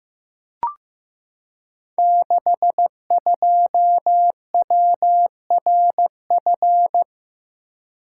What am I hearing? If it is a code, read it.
Morse code, heard as a keyed tone, about 15 wpm, 700 Hz. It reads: 62WRF
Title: Code Groups - 5 Letter:Numbers - 15wpm